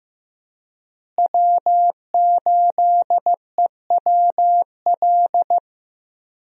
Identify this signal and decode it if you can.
Morse code, heard as a keyed tone, about 15 wpm, 700 Hz. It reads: W8EWL
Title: USA Call Signs - 5 Characters - Repeated 3x - 15wpm